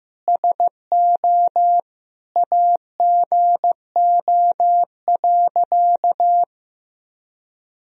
Morse code, heard as a keyed tone, about 15 wpm, 700 Hz. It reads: SO AGO.
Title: Sentences from Top 1000 Words - Review Part 3 - 15wpm